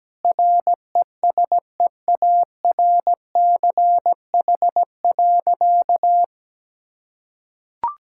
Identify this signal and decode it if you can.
Morse code, heard as a keyed tone, about 17 wpm, 700 Hz. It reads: RESEARCH.
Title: Sentences from Top 1000 Words - Review Part 5 - 17wpm